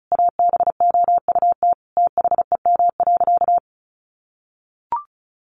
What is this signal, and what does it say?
Morse code, heard as a keyed tone, about 34 wpm, 700 Hz. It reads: ABOUT THEM.
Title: Sentences from Top 800 Words 35wpm